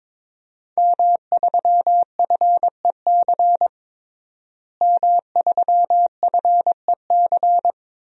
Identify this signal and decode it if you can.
Morse code, heard as a keyed tone, about 22 wpm, 700 Hz. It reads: M3FEC M3FEC
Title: DXCC-100 Call Signs - 5 Characters - Repeated 2x - 22wpm